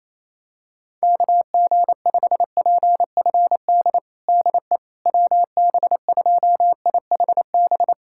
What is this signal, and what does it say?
Morse code, heard as a keyed tone, about 28 wpm, 700 Hz. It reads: KG5PFD DE WB2IHB
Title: USA Callsign DE Callsign 28wpm